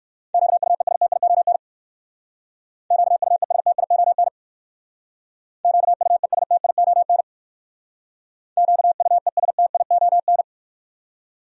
Morse code, heard as a keyed tone, about 59 wpm, 700 Hz. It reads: QUESTION QUESTION QUESTION QUESTION